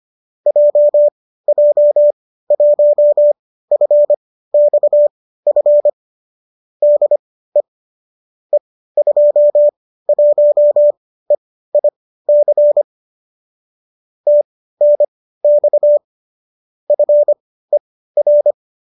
Morse code, heard as a keyed tone, about 25 wpm, spaced as for 15 wpm, 590 Hz. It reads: JJ1FXF DE E21EIC TNX FER
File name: qso208 15